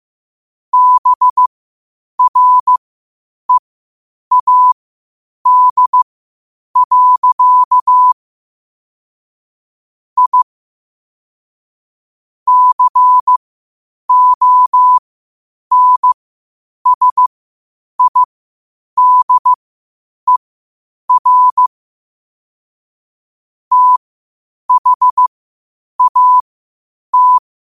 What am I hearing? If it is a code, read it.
Morse code, heard as a keyed tone, about 15 wpm, spaced as for 8 wpm, 1000 Hz. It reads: BREAD. I CONSIDER THAT
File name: qotdfiles / 8wpm